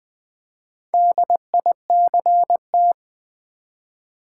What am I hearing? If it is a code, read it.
Morse code, heard as a keyed tone, about 20 wpm, 700 Hz. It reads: DICT
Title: Build A Word - 2 Letters At A Time - 8-Letter Words - 20wpm